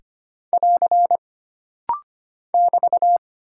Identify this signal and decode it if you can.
Morse code, heard as a keyed tone, about 25 wpm, 700 Hz. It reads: <AR> <BT>
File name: Single Character (Letter:Number:Punctuation:Prosign) - Rapid-Fire - 25wpm